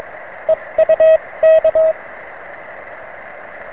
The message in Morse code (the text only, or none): EU K